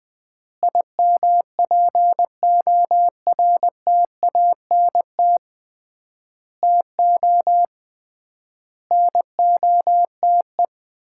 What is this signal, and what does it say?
Morse code, heard as a keyed tone, about 20 wpm, 700 Hz. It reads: IMPORTANT TO NOTE
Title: 3x Word-Spacing - Sets of 5 Words - 20wpm